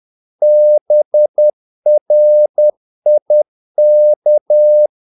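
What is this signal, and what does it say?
Morse code, heard as a keyed tone, about 10 wpm, 600 Hz. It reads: BRIK
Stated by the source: Morse practice of short Danish words at 10 wpm, from CW med Gnister.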